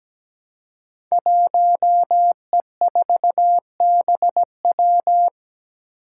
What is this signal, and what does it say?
Morse code, heard as a keyed tone, about 17 wpm, 700 Hz. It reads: 1E4BW
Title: Five Challenging Letter-Number Combinations 17wpm